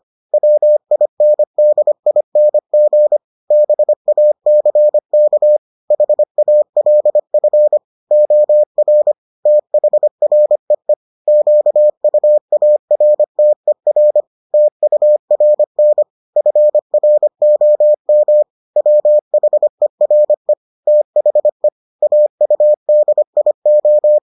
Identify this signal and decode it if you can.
Morse code, heard as a keyed tone, about 25 wpm, 600 Hz. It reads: WINDING BACK HALF OR THREE QUARTER TURN FROM WHERE THE AUDIO